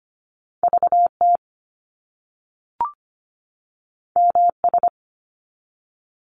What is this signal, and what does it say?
Morse code, heard as a keyed tone, about 25 wpm, 700 Hz. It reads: VT MS
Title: US States 25wpm